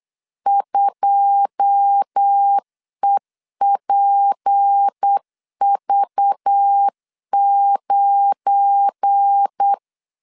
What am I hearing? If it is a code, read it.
Morse code, heard as a keyed tone, about 8 wpm, 800 Hz. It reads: 2EPV9
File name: CW / DET / 1 9 1500C S C 8wpm 5653